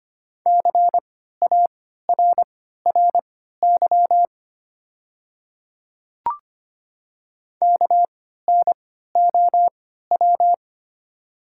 Morse code, heard as a keyed tone, about 25 wpm, spaced as for 14 wpm, 690 Hz. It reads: CARRY KNOW
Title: 3x Character Spacing - Top 500 Words - 25wpm